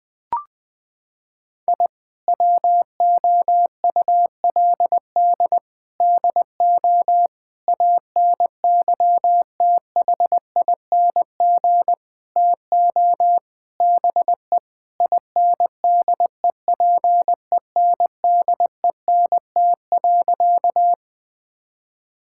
Morse code, heard as a keyed tone, about 20 wpm, 700 Hz. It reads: I WOULD DO ANYTHING TO BE INDEPENDENT.
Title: Sentences from Top 1000 Words - Review Part 6 - 20wpm